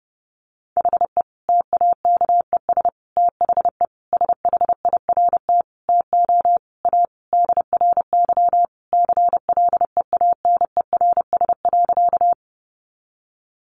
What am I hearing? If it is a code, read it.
Morse code, heard as a keyed tone, about 30 wpm, 700 Hz. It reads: HE TAKES THE SHIRT TO A DRY CLEANERS.
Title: Additional Easy Phrases 30wpm